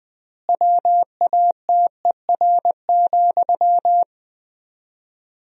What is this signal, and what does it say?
Morse code, heard as a keyed tone, about 20 wpm, 700 Hz. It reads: WATER,
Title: 4x Word-Spacing - Sentences from Top 400 Words - 20wpm